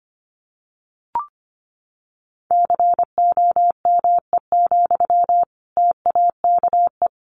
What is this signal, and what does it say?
Morse code, heard as a keyed tone, about 25 wpm, 700 Hz. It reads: COME, TAKE
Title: Sentences from Top 1200 Words 25wpm